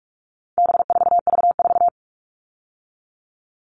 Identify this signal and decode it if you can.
Morse code, heard as a keyed tone, about 44 wpm, 700 Hz. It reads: B4V4
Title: All Cut Numbers - 4-Digit 45wpm